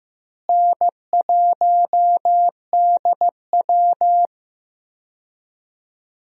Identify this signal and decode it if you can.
Morse code, heard as a keyed tone, about 16 wpm, 700 Hz. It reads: N1DW
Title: USA Call Signs - 4 Characters - Repeated 2x - 15wpm